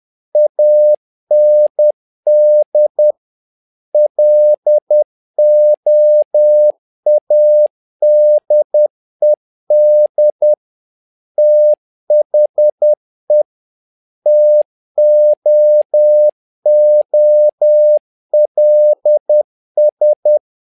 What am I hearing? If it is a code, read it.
Morse code, heard as a keyed tone, about 10 wpm, 600 Hz. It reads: AND LOADED THE TOOLS
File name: learning center / tower trouble / tower trouble 10wpm